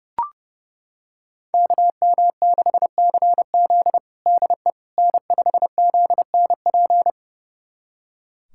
Morse code, heard as a keyed tone, about 31 wpm, 700 Hz. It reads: KM6CZ DE N5ZNP
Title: USA Callsign DE Callsign - 5 Characters - 30wpm